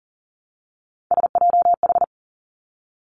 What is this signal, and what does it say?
Morse code, heard as a keyed tone, about 41 wpm, 700 Hz. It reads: SJH
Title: Three Challenging Letter-Number Combinations 40wpm